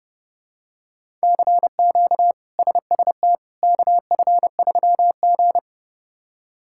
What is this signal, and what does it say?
Morse code, heard as a keyed tone, about 30 wpm, 700 Hz. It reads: CQ SST KF3G
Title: K1USN SST QSO Practice - 30wpm